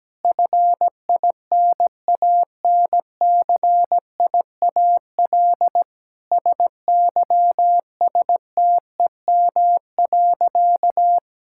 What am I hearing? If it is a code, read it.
Morse code, heard as a keyed tone, about 17 wpm, 700 Hz. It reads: FINANCIAL SYSTEM.
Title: Sentences from Top 1000 Words - Review Part 5 - 17wpm